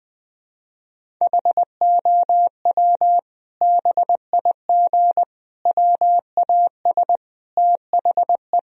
Morse code, heard as a keyed tone, about 20 wpm, 700 Hz. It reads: HOW BIG WAS THE
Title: Sentences from Top 1000 Words - Review Part 4 - 20wpm